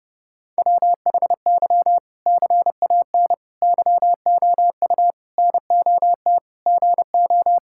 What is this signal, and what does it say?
Morse code, heard as a keyed tone, about 30 wpm, 700 Hz. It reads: WHY CAN YOU NOT GO
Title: Sentences from Top 1600 Words 30wpm